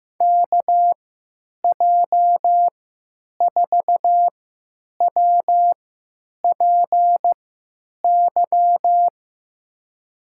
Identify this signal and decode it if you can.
Morse code, heard as a keyed tone, about 15 wpm, spaced as for 9 wpm, 700 Hz. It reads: KJ4WPY
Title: USA Call Signs - 6 Characters - 3x Character Spacing - 15wpm